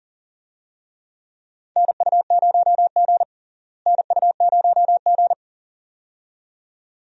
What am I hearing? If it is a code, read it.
Morse code, heard as a keyed tone, about 40 wpm, 700 Hz. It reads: NU0G NU0G
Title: USA Call Signs - 4 Characters - Repeated 2x - 40wpm